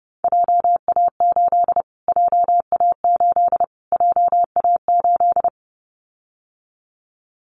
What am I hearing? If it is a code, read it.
Morse code, heard as a keyed tone, about 30 wpm, 700 Hz. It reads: JA8 JA8 JA8
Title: Callsign Prefixes for Top 25 DXCC - Repeated 3x - 30wpm